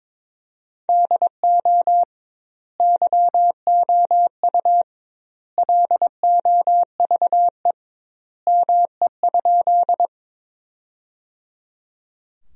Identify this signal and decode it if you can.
Morse code, heard as a keyed tone, about 21 wpm, 700 Hz. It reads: DO YOU LOVE ME?
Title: 2x Word-Spacing - Sentences from Top 400 Words - 22wpm